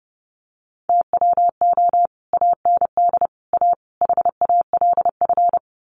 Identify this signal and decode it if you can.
Morse code, heard as a keyed tone, about 30 wpm, 700 Hz. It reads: TWO AND A HALF
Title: Sets of 4 Words 30wpm